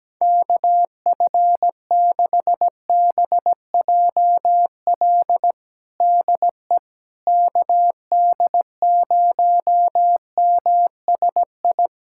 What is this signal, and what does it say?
Morse code, heard as a keyed tone, about 17 wpm, 700 Hz. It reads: KF6BJL DE KD0MSI
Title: USA Callsign DE Callsign - 6 Characters - 17wpm